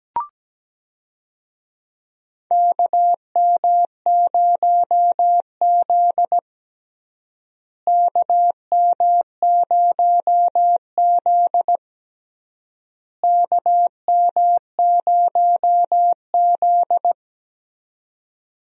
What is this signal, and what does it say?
Morse code, heard as a keyed tone, about 17 wpm, 700 Hz. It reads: KM0Z KM0Z KM0Z
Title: USA Call Signs - 4 Characters - Repeated 3x - 17wpm